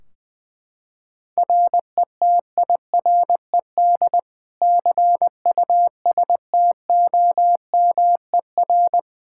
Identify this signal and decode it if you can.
Morse code, heard as a keyed tone, about 20 wpm, 700 Hz. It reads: RETIRED CUSTOMER
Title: Practice QSO - Exchange 3 - More Info - 20wpm